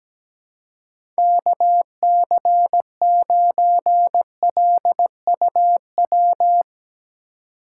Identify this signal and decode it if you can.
Morse code, heard as a keyed tone, about 17 wpm, 700 Hz. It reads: KC9LUW
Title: USA Call Signs - 6 Characters - 3x Character Spacing - 17wpm